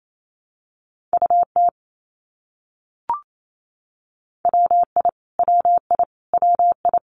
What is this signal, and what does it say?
Morse code, heard as a keyed tone, about 28 wpm, 700 Hz. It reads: UT WI WI WI
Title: US States Repeated 3x 28wpm